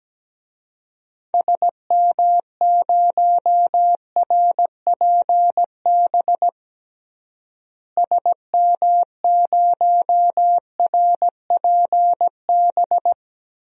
Morse code, heard as a keyed tone, about 17 wpm, 700 Hz. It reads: SM0RPB SM0RPB
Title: DXCC-100 Call Signs - 6 Characters - Repeated 3x - 17wpm